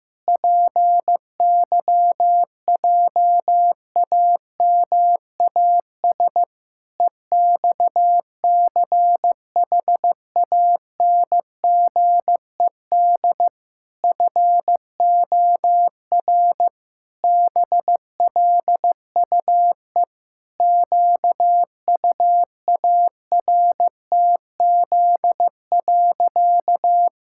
Morse code, heard as a keyed tone, about 15 wpm, 700 Hz. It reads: PYJAMAS EXCHANGED FOR BLUE QUARTZ.
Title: Pangram Challenge v2 - 15wpm